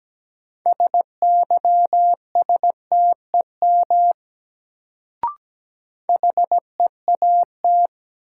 Morse code, heard as a keyed tone, about 17 wpm, 700 Hz. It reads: SYSTEM HEAT
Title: Sentences from Top 1000 Words - Review Part 3 - New Word Drill - 17wpm